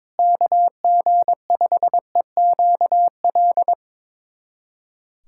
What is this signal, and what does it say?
Morse code, heard as a keyed tone, about 22 wpm, 700 Hz. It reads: KG5EQL
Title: USA Callsign DE Callsign - 6 Characters - 22wpm